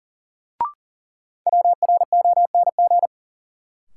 Morse code, heard as a keyed tone, about 40 wpm, 700 Hz. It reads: WRONG
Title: Sets of 5 Words Encore - New Word Drill - 40wpm